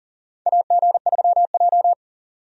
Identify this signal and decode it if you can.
Morse code, heard as a keyed tone, about 39 wpm, 700 Hz. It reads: AG3J
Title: USA Call Signs - 4 Characters - 40wpm